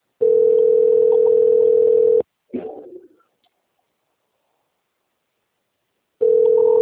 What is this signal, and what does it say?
Ringback tone (North America), 440 + 480 Hz.
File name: out- - - - - .68584